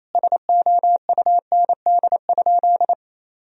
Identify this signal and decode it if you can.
Morse code, heard as a keyed tone, about 28 wpm, 700 Hz. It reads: SOUND?
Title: Sentences from Top 1000 Words - Review Part 1 - 28wpm